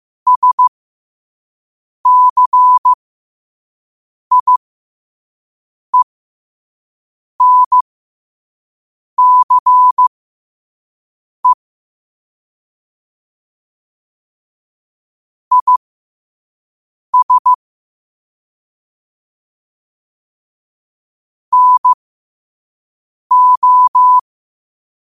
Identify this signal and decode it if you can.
Morse code, heard as a keyed tone, about 15 wpm, spaced as for 5 wpm, 1000 Hz. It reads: SCIENCE IS NO
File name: qotdfiles / 5wpm